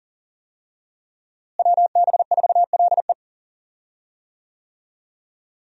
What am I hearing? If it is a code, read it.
Morse code, heard as a keyed tone, about 40 wpm, 700 Hz. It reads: WB4LE
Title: USA Call Signs - 5 Characters - Repeated 2x - 40wpm